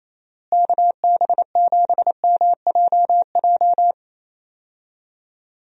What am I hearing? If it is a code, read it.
Morse code, heard as a keyed tone, about 28 wpm, 700 Hz. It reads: KB7MJJ
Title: USA Call Signs - 6 Characters - Repeated 3x - 28wpm